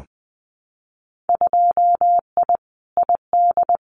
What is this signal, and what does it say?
Morse code, heard as a keyed tone, about 20 wpm, 700 Hz. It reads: 2I ID
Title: Winter Field Day Exchanges Encore 20wpm